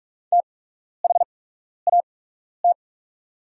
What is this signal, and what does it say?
Morse code, heard as a keyed tone, about 46 wpm, spaced as for 12 wpm, 700 Hz. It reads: THAT
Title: 8x Character Spacing - Top 100 Words - 45wpm